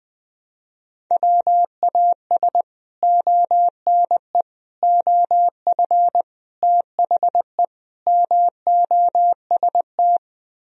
Morse code, heard as a keyed tone, about 20 wpm, 700 Hz. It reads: WAS ONE OF THE MOST
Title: Speed-Racing - Sets of 5 Words 17wpm